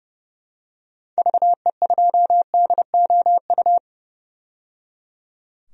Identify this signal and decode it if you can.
Morse code, heard as a keyed tone, about 31 wpm, 700 Hz. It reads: VE2DOU
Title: DXCC-100 Call Signs - 6 Characters - 30wpm